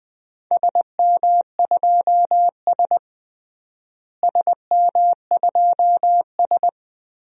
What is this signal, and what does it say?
Morse code, heard as a keyed tone, about 21 wpm, 700 Hz. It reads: SM2S SM2S
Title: DXCC-100 Call Signs - 4 Characters - Repeated 3x - 20wpm